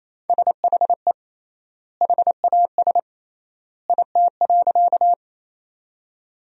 Morse code, heard as a keyed tone, about 28 wpm, 700 Hz. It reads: SHE HAS IT.
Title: 3x Word-Spacing - Sentences from Top 100 Words - 28wpm